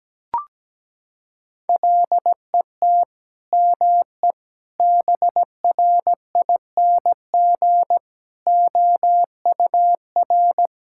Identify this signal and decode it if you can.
Morse code, heard as a keyed tone, about 17 wpm, 700 Hz. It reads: LET ME BRING OUR
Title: Sentences from Top 1000 Words - Review Part 6 - 17wpm